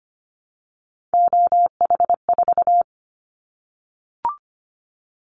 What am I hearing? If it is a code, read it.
Morse code, heard as a keyed tone, about 25 wpm, 700 Hz. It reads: OH4
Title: Callsign Prefixes for Top 50 DXCC - 25wpm